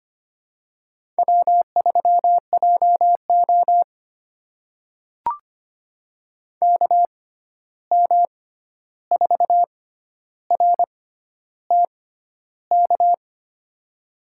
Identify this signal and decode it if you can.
Morse code, heard as a keyed tone, about 25 wpm, spaced as for 11 wpm, 700 Hz. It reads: W3JO KM4RTK
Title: USA Call Signs - 6x Character Spacing - 25wpm